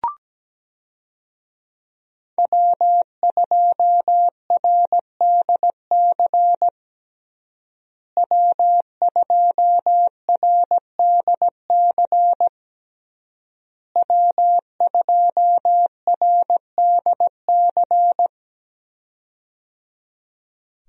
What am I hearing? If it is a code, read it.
Morse code, heard as a keyed tone, about 17 wpm, 700 Hz. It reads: W2RDC W2RDC W2RDC
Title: USA Call Signs - 5 Characters - Repeated 3x - 17wpm